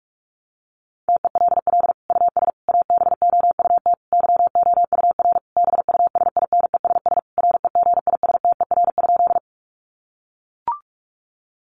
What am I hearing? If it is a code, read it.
Morse code, heard as a keyed tone, about 44 wpm, 700 Hz. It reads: TELL US ABOUT YOUR BUSINESS REGISTER?
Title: Sentences from Top 1200 Words 45wpm